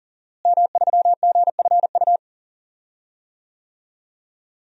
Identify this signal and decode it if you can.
Morse code, heard as a keyed tone, about 40 wpm, 700 Hz. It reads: M3GFU
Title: DXCC-100 Call Signs - 5 Characters - Repeated 2x - 40wpm